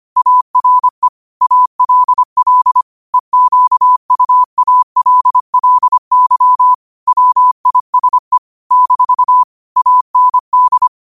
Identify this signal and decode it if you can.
Morse code, heard as a keyed tone, about 25 wpm, 1000 Hz. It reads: ARE ALL EQUALLY WISE - AND